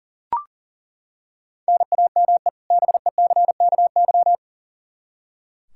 Morse code, heard as a keyed tone, about 40 wpm, 700 Hz. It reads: NAME BECKY
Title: K1USN SST QSO Practice 40wpm